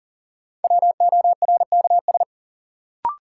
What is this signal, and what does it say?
Morse code, heard as a keyed tone, about 40 wpm, 700 Hz. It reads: WORKS
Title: Sentences from Top 1000 Words - Review Part 5 - New Word Drill - 40wpm